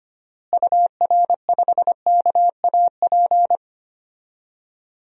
Morse code, heard as a keyed tone, about 25 wpm, 700 Hz. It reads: UR5KAP
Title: DXCC-100 Call Signs - 6 Characters - Repeated 3x - 25wpm